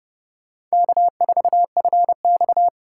Morse code, heard as a keyed tone, about 36 wpm, 700 Hz. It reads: K4FX
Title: Speed Racing - USA 1x2 Call Signs 30wpm